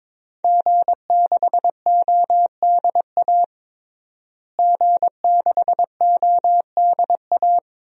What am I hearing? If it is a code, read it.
Morse code, heard as a keyed tone, about 22 wpm, 700 Hz. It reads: G6ODA G6ODA
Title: DXCC-100 Call Signs - 5 Characters - Repeated 2x - 22wpm